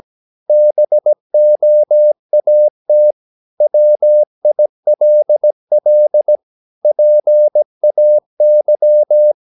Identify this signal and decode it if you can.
Morse code, heard as a keyed tone, about 17 wpm, 600 Hz. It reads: BOAT WILL PAY
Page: Romanen Never af Ken Follet på engelsk.